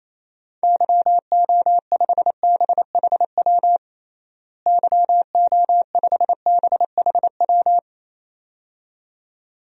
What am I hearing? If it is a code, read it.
Morse code, heard as a keyed tone, about 28 wpm, 700 Hz. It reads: YO5BHW YO5BHW
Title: DXCC-100 Call Signs - 6 Characters - Repeated 2x - 28wpm